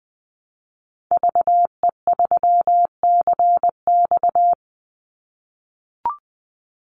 Morse code, heard as a keyed tone, about 20 wpm, 700 Hz. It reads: VE3CX
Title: Call Signs - 20wpm